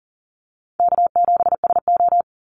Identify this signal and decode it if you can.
Morse code, heard as a keyed tone, about 40 wpm, 700 Hz. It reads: K7SO
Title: NA SOTA Activator Callsigns 40wpm